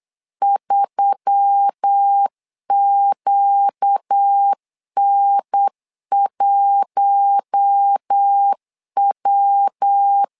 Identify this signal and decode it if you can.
Morse code, heard as a keyed tone, about 8 wpm, 800 Hz. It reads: 3QN1W
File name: LEZIONI / 4 Q1S3 8wpm 3037